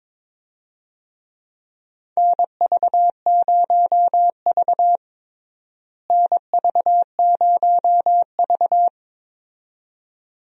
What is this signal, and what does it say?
Morse code, heard as a keyed tone, about 22 wpm, 700 Hz. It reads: NV0V NV0V
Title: USA Call Signs - 4 Characters - Repeated 2x - 22wpm